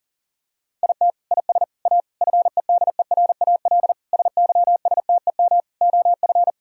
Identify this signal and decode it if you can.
Morse code, heard as a keyed tone, about 37 wpm, 700 Hz. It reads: IT IS A FEDERAL SYSTEM OF
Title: Sentences from Top 1000 Words - Review Part 5 - 40wpm